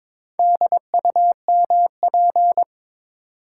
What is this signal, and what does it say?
Morse code, heard as a keyed tone, about 22 wpm, 700 Hz. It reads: DUMP
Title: Sentences from Top 1300 Words - New Word Drill - 22wpm